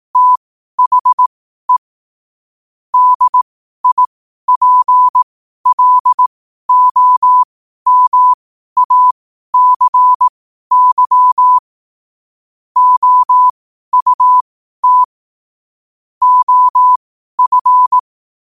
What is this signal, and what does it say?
Morse code, heard as a keyed tone, about 18 wpm, spaced as for 12 wpm, 1000 Hz. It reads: THE DIPLOMACY OUT OF
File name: qotdfiles / 12wpm